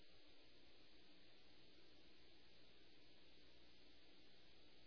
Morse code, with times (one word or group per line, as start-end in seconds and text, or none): none